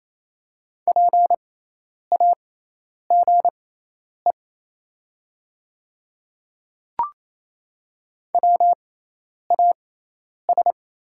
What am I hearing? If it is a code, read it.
Morse code, heard as a keyed tone, about 27 wpm, spaced as for 10 wpm, 700 Hz. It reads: PAGE WAS
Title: 6x Character Spacing - Top 300 Words - 28wpm